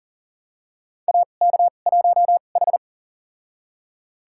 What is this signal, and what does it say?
Morse code, heard as a keyed tone, about 40 wpm, spaced as for 29 wpm, 700 Hz. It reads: AK1H
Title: USA Call Signs - 4 Characters - 2x Character Spacing - 40wpm